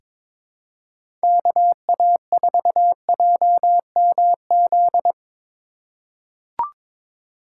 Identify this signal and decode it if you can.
Morse code, heard as a keyed tone, about 22 wpm, 700 Hz. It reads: KA4JMZ
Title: USA Call Signs - 6 Characters - 4x Character Spacing - 22wpm